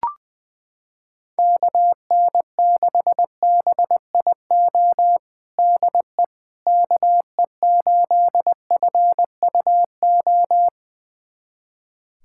Morse code, heard as a keyed tone, about 20 wpm, 700 Hz. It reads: KN6BIO DE KE8FUO
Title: USA Callsign DE Callsign - 6 Characters - 20wpm